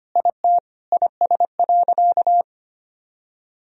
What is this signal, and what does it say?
Morse code, heard as a keyed tone, about 25 wpm, 700 Hz. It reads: IT IS.
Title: Sentences from Top 1000 Words - Review Part 3 - 25wpm